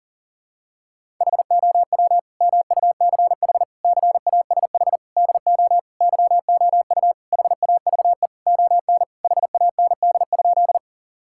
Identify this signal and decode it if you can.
Morse code, heard as a keyed tone, about 40 wpm, 700 Hz. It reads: HOW MUCH CASH DO YOU HAVE ON HAND?
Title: Sentences from Top 1000 Words - Review Part 5 - 40wpm